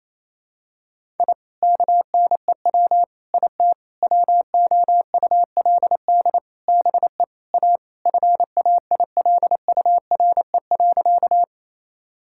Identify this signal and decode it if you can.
Morse code, heard as a keyed tone, about 28 wpm, 700 Hz. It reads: I KNEW IT WOULD BE A FAILURE.
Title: Sentences from Top 1000 Words - Review Part 6 - 28wpm